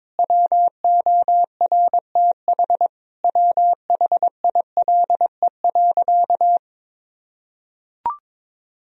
Morse code, heard as a keyed tone, about 22 wpm, 700 Hz. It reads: WORTH WHILE.
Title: Sentences from Top 1000 Words - Review Part 1 - 22wpm